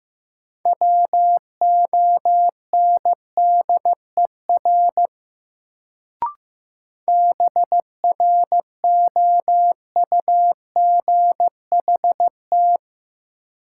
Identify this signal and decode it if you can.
Morse code, heard as a keyed tone, about 15 wpm, 700 Hz. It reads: WONDER BROUGHT
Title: Sentences from Top 500 Words - New Word Drill - 15wpm